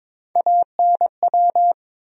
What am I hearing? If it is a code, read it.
Morse code, heard as a keyed tone, about 22 wpm, 700 Hz. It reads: ANW
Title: USA Call Sign Postfixes - 3 Letters 22wpm